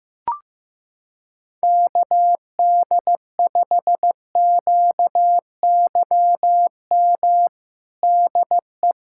Morse code, heard as a keyed tone, about 15 wpm, 700 Hz. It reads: KD5QYM DE
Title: USA Callsign DE Callsign - 6 Characters - 15wpm